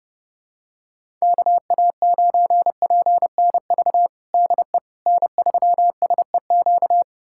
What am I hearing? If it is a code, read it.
Morse code, heard as a keyed tone, about 29 wpm, 700 Hz. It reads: KA9PNV DE N3SEQ
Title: USA Callsign DE Callsign 30wpm